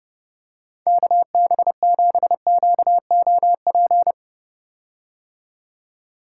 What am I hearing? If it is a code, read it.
Morse code, heard as a keyed tone, about 30 wpm, 700 Hz. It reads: KB7QOP
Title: K1USN SST QSO Practice - 30wpm